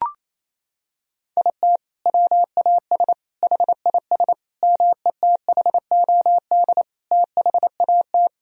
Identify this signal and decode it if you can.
Morse code, heard as a keyed tone, about 28 wpm, 700 Hz. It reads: IT WAS HIS METHOD THAT
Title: Sentences from Top 1000 Words - Review Part 3 - 28wpm